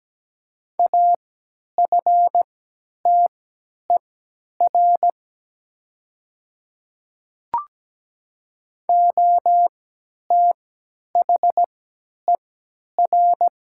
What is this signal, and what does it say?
Morse code, heard as a keyed tone, about 17 wpm, spaced as for 10 wpm, 700 Hz. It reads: AFTER OTHER
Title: 3x Character Spacing - Top 200 Words - 17wpm